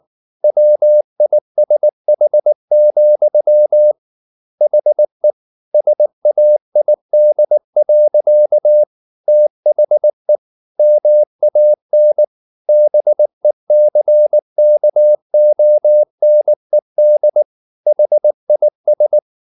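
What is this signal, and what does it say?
Morse code, heard as a keyed tone, about 19 wpm, 600 Hz. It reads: WISH, HE SAID. THE MAN BECKONED HIS
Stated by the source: Morse code at 19 wpm from CW med Gnister, reading Ken Follett's novel Never in English.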